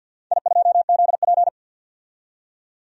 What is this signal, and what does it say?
Morse code, heard as a keyed tone, about 52 wpm, 700 Hz. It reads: I2ZP
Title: DXCC-100 Call Signs - 4 Characters - 50wpm